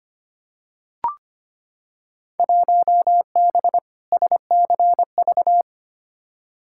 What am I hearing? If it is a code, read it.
Morse code, heard as a keyed tone, about 25 wpm, 700 Hz. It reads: 1B SCV
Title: Field Day Exchanges - 25wpm